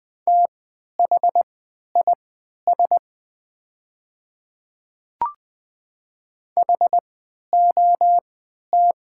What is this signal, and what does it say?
Morse code, heard as a keyed tone, about 20 wpm, spaced as for 11 wpm, 700 Hz. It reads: THIS HOT
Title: 3x Character Spacing - Top 100 Words - 20wpm